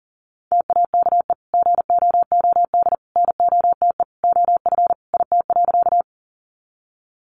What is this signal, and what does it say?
Morse code, heard as a keyed tone, about 40 wpm, 700 Hz. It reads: TAKE GOOD NOTE OF IT.
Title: Sentences from Top 500 Words 40wpm